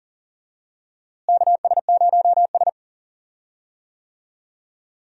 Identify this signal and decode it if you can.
Morse code, heard as a keyed tone, about 40 wpm, 700 Hz. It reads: KS0S
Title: USA Call Signs - 4 Characters - Repeated 2x - 40wpm